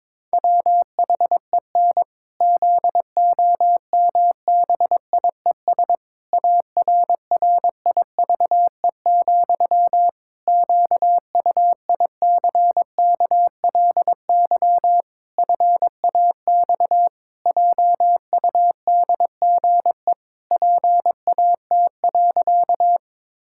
Morse code, heard as a keyed tone, about 22 wpm, 700 Hz. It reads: WHEN ZOMBIES ARRIVE, QUICKLY FAX JUDGE PAT.
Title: Pangram Challenge v2 - 22wpm